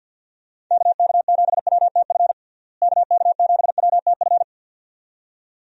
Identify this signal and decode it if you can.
Morse code, heard as a keyed tone, about 52 wpm, 700 Hz. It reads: KK7WTF KK7WTF
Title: USA Call Signs - 6 Characters - Repeated 2x - 50wpm